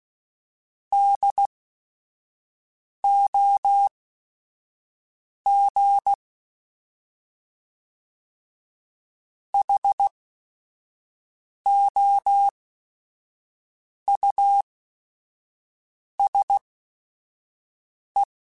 Morse code, heard as a keyed tone, about 16 wpm, spaced as for 5 wpm, 800 Hz. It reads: DOG HOUSE
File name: General / General-mp3s / Houses @5